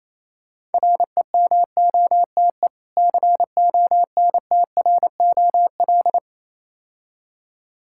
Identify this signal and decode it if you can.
Morse code, heard as a keyed tone, about 27 wpm, 690 Hz. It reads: REMOTE CONTROL
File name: Easy Words - Speed-Racing with Hints - 22wpm